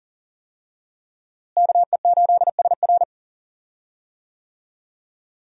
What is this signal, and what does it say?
Morse code, heard as a keyed tone, about 40 wpm, 700 Hz. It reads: KE8SR
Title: USA Call Signs - 5 Characters - Repeated 3x - 40wpm